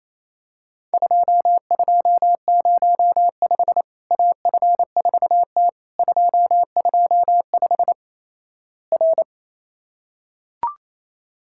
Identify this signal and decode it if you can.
Morse code, heard as a keyed tone, about 28 wpm, 700 Hz. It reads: 2205 AF4T 225 R
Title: Worked All Europe DX Contest - QTC Exchange - 28wpm